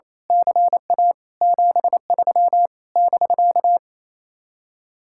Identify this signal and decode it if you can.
Morse code, heard as a keyed tone, about 27 wpm, 700 Hz. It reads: CA 73 <BK>
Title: Speed-Racing Common QSO Phrases Encore - 28wpm